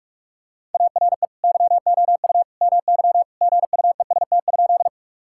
Morse code, heard as a keyed tone, about 46 wpm, 700 Hz. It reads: ARE YOU MY GUEST?
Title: Sentences from Top 1600 Words 45wpm